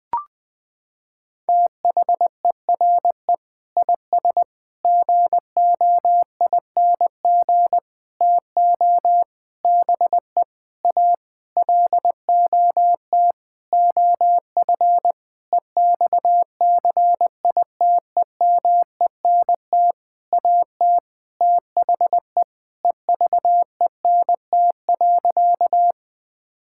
Morse code, heard as a keyed tone, about 20 wpm, 700 Hz. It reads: THERE IS GOING TO BE A LOT OF EXCITEMENT AT THE EVENT.
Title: Sentences with 7-Word Phrases - 20wpm